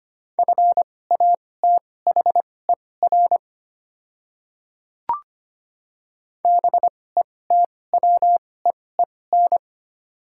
Morse code, heard as a keyed tone, about 25 wpm, spaced as for 18 wpm, 700 Hz. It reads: FATHER BETWEEN
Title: 2x Character Spacing - Top 500 Words - 25wpm